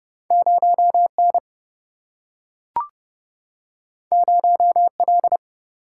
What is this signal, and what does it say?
Morse code, heard as a keyed tone, about 30 wpm, 700 Hz. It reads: 0N 0L
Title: Single-Letter Single-Number 30wpm